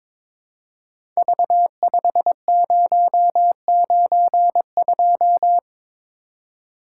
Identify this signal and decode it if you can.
Morse code, heard as a keyed tone, about 22 wpm, 700 Hz. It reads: V5092
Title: Code Groups - 5 Letter:Numbers - 22wpm